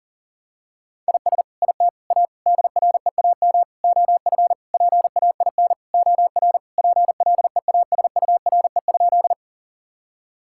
Morse code, heard as a keyed tone, about 37 wpm, 700 Hz. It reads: IS IT A DREAM OF PAIN OR PLEASURE?
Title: Sentences from Top 1600 Words 40wpm